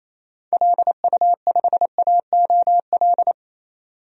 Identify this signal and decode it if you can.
Morse code, heard as a keyed tone, about 28 wpm, 700 Hz. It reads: LU5AOL
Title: DXCC-100 Call Signs - 6 Characters - 28wpm